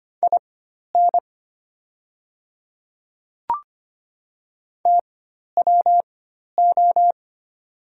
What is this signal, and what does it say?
Morse code, heard as a keyed tone, about 25 wpm, spaced as for 12 wpm, 700 Hz. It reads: IN TWO
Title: 4x Character Spacing - Top 100 Words - 25wpm